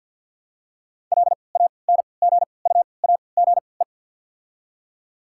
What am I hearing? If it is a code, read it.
Morse code, heard as a keyed tone, about 48 wpm, spaced as for 28 wpm, 700 Hz. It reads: LANGUAGE
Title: 3x Character Spacing - Top 500 Words - 50wpm